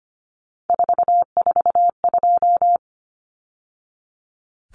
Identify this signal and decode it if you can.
Morse code, heard as a keyed tone, about 25 wpm, 700 Hz. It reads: 442
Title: Numbers - 3-Digit 25wpm